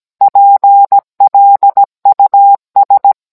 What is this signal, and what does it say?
Morse code, heard as a keyed tone, about 17 wpm, 800 Hz. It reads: PLUS